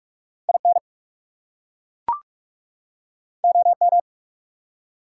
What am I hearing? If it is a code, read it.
Morse code, heard as a keyed tone, about 41 wpm, 700 Hz. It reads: IN OM
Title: Top 180 QSO Elements 45wpm